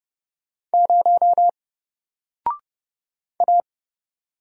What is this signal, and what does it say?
Morse code, heard as a keyed tone, about 31 wpm, 700 Hz. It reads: 0 A
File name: Single-Letter-Number-30wpm